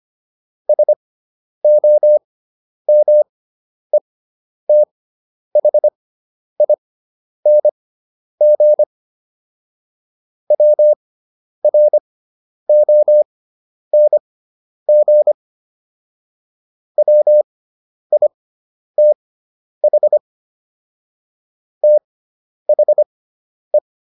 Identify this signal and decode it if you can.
Morse code, heard as a keyed tone, about 25 wpm, spaced as for 10 wpm, 600 Hz. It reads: SOMETHING WRONG WITH THE